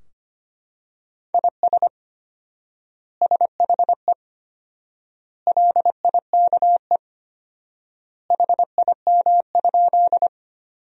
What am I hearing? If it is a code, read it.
Morse code, heard as a keyed tone, about 25 wpm, 700 Hz. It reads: IS SHE LIKE HIM?
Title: 4x Word-Spacing - Sentences from Top 100 Words - 25wpm